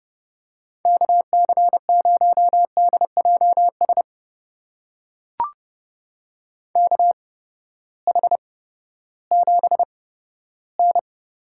Morse code, heard as a keyed tone, about 32 wpm, spaced as for 15 wpm, 700 Hz. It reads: KC0DJS KH7N
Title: USA Call Signs - 8x Character Spacing - 30wpm